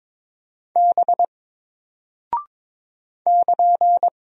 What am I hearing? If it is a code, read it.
Morse code, heard as a keyed tone, about 22 wpm, 700 Hz. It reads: B <KN>
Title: Single Character (Letter:Number:Punctuation:Prosign) Alt 22wpm